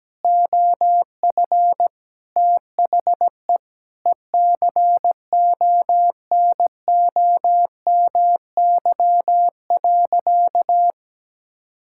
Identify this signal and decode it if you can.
Morse code, heard as a keyed tone, about 17 wpm, 700 Hz. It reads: OF THE ECONOMY.
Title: Sentences from Top 1000 Words - Review Part 6 - 17wpm